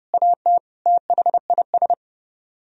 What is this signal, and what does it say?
Morse code, heard as a keyed tone, about 39 wpm, 700 Hz. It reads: AT THIS